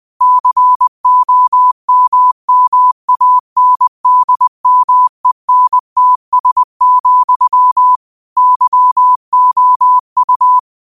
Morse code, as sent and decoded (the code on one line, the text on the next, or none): -.-. --- -- -- .- -. -.. -- . -. - ... --..-- / -.-- --- ..-
COMMANDMENTS, YOU